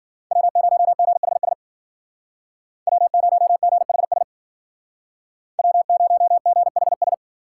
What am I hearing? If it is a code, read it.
Morse code, heard as a keyed tone, about 50 wpm, 700 Hz. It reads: W0GHS W0GHS W0GHS